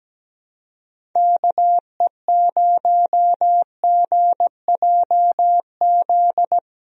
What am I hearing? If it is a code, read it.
Morse code, heard as a keyed tone, about 17 wpm, 700 Hz. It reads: KE0GJZ
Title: USA Call Signs - 6 Characters - Repeated 2x - 17wpm